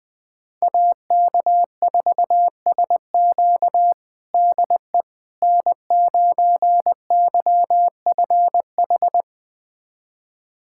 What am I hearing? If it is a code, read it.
Morse code, heard as a keyed tone, about 20 wpm, 700 Hz. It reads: AK4SQ DE N9YFH
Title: USA Callsign DE Callsign - 5 Characters - 20wpm